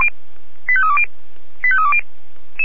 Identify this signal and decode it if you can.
Five-tone selective call (ZVEI), heard as quick runs of five tones, tones of about 70 ms. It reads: 85320 85320
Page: SDR/ZVEI